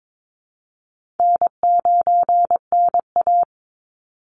Morse code, heard as a keyed tone, about 22 wpm, 700 Hz. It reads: N9NA
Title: Easy Call Signs - 22wpm